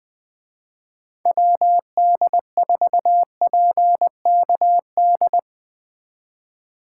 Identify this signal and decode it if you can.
Morse code, heard as a keyed tone, about 20 wpm, 700 Hz. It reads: WD4PKD
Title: Speed Racing - USA Call Signs 20wpm